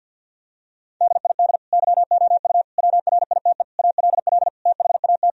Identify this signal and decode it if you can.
Morse code, heard as a keyed tone, about 50 wpm, 700 Hz. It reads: DID YOU WRITE ALL THAT
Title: Sentences from Top 1000 Words - Review Part 1 - 50wpm